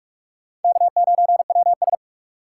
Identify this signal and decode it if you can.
Morse code, heard as a keyed tone, about 43 wpm, 700 Hz. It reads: K9WS
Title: USA Call Signs - 4 Characters - 8x Character Spacing - 45wpm